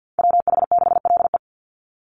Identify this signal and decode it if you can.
Morse code, heard as a keyed tone, about 50 wpm, 700 Hz. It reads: WH6LE
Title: NA SOTA Activator Callsigns 50wpm